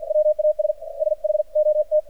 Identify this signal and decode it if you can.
Morse code, heard as a keyed tone, about 48 wpm, 610 Hz. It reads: WAS SHOT
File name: archived / CRSnet